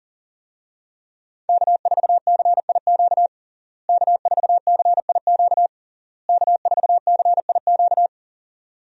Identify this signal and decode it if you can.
Morse code, heard as a keyed tone, about 40 wpm, 700 Hz. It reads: K4CIQ K4CIQ K4CIQ
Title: USA Call Signs - 5 Characters - Repeated 3x - 40wpm